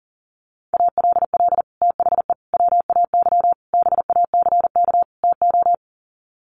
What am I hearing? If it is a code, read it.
Morse code, heard as a keyed tone, about 40 wpm, 700 Hz. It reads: ALL THE WAY BACK TO
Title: Sets of 5 Words Encore 40wpm